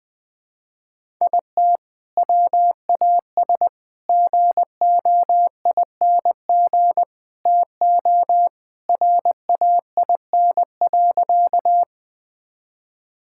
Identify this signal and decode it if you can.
Morse code, heard as a keyed tone, about 20 wpm, 700 Hz. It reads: IT WAS GOING TO RAIN.
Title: Sentences with 4-Word Phrases - Spoken Beginning - 20wpm